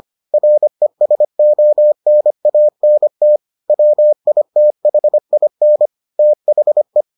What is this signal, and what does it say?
Morse code, heard as a keyed tone, about 25 wpm, 600 Hz. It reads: RESONANT WITHIN THE